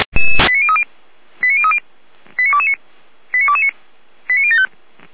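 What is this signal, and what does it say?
Five-tone selective call (ZVEI), heard as quick runs of five tones, tones of about 70 ms. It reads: 89030 89030 89229 89229 89075
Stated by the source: ZVEI five-tone selective call received on an SDR.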